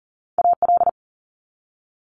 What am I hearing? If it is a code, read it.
Morse code, heard as a keyed tone, about 42 wpm, 700 Hz. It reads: AL
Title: Multi-letter Phonemes 40wpm